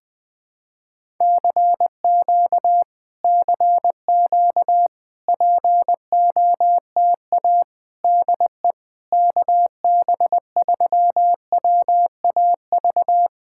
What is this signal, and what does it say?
Morse code, heard as a keyed tone, about 20 wpm, 700 Hz. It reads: CQ CQ POTA DE KB3WAV
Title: POTA QSO Practice - 20wpm